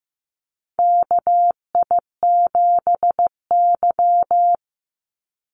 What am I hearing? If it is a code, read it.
Morse code, heard as a keyed tone, about 15 wpm, 700 Hz. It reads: KI7Y
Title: Call Signs Repeated 2x - 15wpm